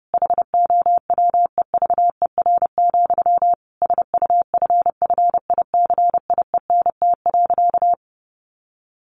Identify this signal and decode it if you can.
Morse code, heard as a keyed tone, about 30 wpm, 700 Hz. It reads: HOWEVER, SUFFICIENT.
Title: Sentences from Top 1400 Words 30wpm